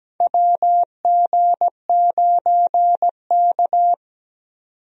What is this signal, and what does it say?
Morse code, heard as a keyed tone, about 17 wpm, 700 Hz. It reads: WG9K
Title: USA Call Signs - 4 Characters - 17wpm